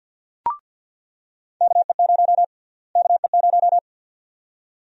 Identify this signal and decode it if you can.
Morse code, heard as a keyed tone, about 49 wpm, 700 Hz. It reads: KE0 KE0
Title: USA Call Sign Prefixes - Repeated 2x - 50wpm